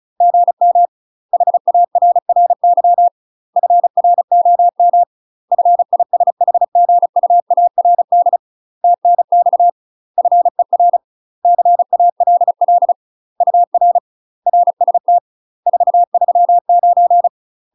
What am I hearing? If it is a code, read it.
Morse code, heard as a keyed tone, about 35 wpm, 700 Hz. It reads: GM HARRY FROM FISHGUARD TNX FER CALL UR RST 439
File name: UK MASTER QSOs TRAINING / 35WPM / UK MASTER QSOs 0044